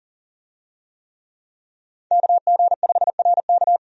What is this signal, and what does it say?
Morse code, heard as a keyed tone, about 40 wpm, 700 Hz. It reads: KG5RK
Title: USA Call Signs - 5 Characters - 40wpm